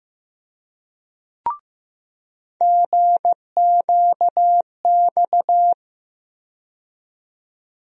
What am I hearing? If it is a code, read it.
Morse code, heard as a keyed tone, about 15 wpm, 700 Hz. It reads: GQX
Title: USA Call Sign Postfixes - 3 Letters 15wpm